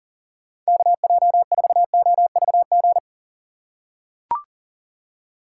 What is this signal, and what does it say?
Morse code, heard as a keyed tone, about 38 wpm, 700 Hz. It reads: KJ4OVG
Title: USA Call Signs - 6 Characters - 2x Character Spacing - 40wpm